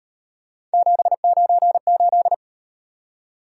Morse code, heard as a keyed tone, about 33 wpm, 700 Hz. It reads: 798